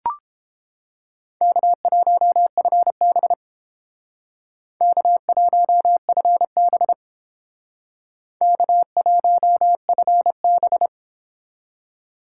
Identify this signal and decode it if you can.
Morse code, heard as a keyed tone, about 29 wpm, 700 Hz. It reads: K1FB K1FB K1FB